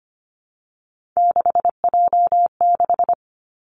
Morse code, heard as a keyed tone, about 25 wpm, 700 Hz. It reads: 6J6
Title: Three Challenging Letter-Number Combinations 25wpm